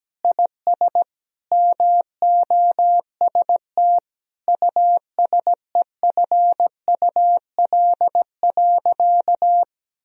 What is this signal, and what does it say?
Morse code, heard as a keyed tone, about 17 wpm, 700 Hz. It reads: IS MOST USEFUL.
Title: Sentences from Top 1000 Words - Review Part 4 - 17wpm